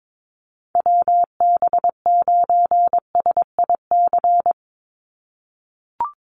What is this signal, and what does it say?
Morse code, heard as a keyed tone, about 22 wpm, 700 Hz. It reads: WB9SIC
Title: USA Call Signs 22wpm